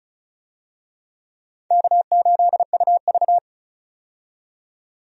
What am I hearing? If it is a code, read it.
Morse code, heard as a keyed tone, about 34 wpm, 700 Hz. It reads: K8UV
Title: USA Call Signs - 4 Characters - 35wpm